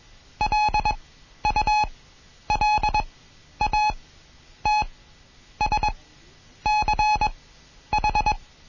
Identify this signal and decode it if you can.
Morse code, heard as a keyed tone, about 21 wpm, spaced as for 10 wpm, 850 Hz. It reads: LULATSCH
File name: Morsen / Kurs A 20